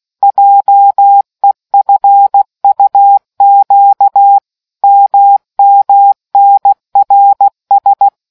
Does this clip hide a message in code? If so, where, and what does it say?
Yes: Morse code from 0.2 to 8.1 s, heard as a keyed tone, about 16 wpm, 800 Hz. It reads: JEFUQ MMNRS